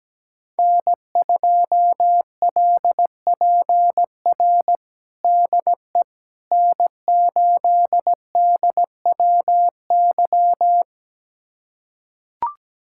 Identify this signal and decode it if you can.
Morse code, heard as a keyed tone, about 17 wpm, 700 Hz. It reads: N2LPR DE N8DWY
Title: USA Callsign DE Callsign - 5 Characters - 17wpm